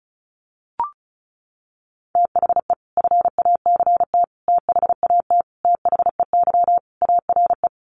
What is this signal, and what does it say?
Morse code, heard as a keyed tone, about 34 wpm, 700 Hz. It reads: THE FACT THAT THEY ARE
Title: Sets of 5 Words Encore2 35wpm